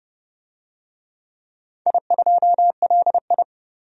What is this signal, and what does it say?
Morse code, heard as a keyed tone, about 30 wpm, 700 Hz. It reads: I2LI
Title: DXCC-100 Call Signs - 4 Characters - 30wpm